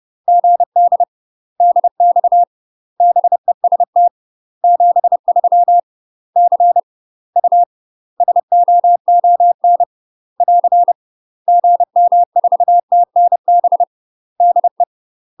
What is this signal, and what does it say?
Morse code, heard as a keyed tone, about 30 wpm, 700 Hz. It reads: GD DX BEST 73 C U SOON <AR> GM4TNB DE
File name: UK MASTER QSOs TRAINING / 30WPM / UK MASTER QSOs 0043